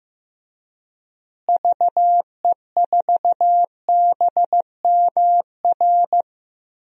Morse code, heard as a keyed tone, about 15 wpm, 700 Hz. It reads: VE4BMR
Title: DXCC-100 Call Signs - 6 Characters - Repeated 3x - 15wpm